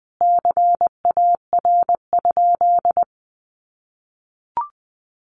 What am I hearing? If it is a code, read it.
Morse code, heard as a keyed tone, about 20 wpm, 700 Hz. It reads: CAR?
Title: Sentences from Top 300 Words 20wpm